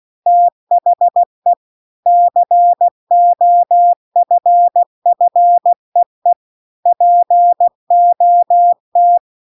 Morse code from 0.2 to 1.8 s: THE